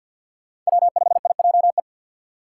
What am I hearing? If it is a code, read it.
Morse code, heard as a keyed tone, about 61 wpm, 700 Hz. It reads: W5IJE
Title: Speed Racing - USA 1x3 Call Signs 50wpm